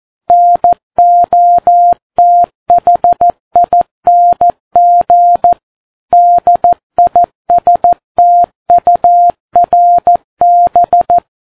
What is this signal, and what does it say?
Morse code, heard as a keyed tone, about 14 wpm, 700 Hz. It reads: NOTHING DISTURB
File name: Ham Radio Sat